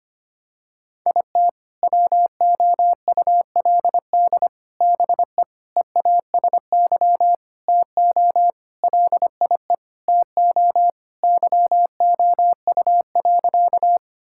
Morse code, heard as a keyed tone, about 25 wpm, 700 Hz. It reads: IT WOULD BE EASY TO LIE TO YOU.
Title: Sentences from Top 1600 Words 25wpm